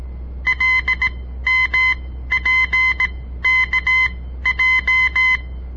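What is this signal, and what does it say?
Morse code, heard as a keyed tone, about 17 wpm, spaced as for 13 wpm, 2050 Hz. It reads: LMPKJ
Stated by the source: On-air Morse code practice broadcast by a southern Montana amateur radio association; all code, no read back.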